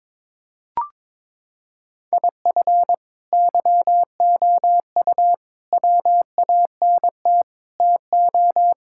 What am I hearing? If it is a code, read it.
Morse code, heard as a keyed tone, about 22 wpm, 700 Hz. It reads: IF YOU WANT TO
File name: practice / 22wpm / Sentences with 4-Word Phrases - 22wpm